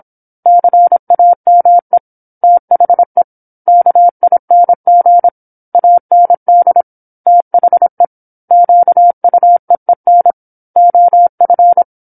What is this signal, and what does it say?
Morse code, heard as a keyed tone, about 27 wpm, 700 Hz. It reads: CAME THE KING AND THE QUEEN OF
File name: 08-Chapter-08 26 WPM